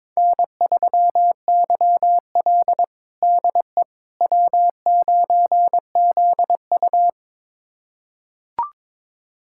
Morse code, heard as a keyed tone, about 22 wpm, 700 Hz. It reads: N3YL DE W9ZU
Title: USA Callsign DE Callsign - 4 Characters - 22wpm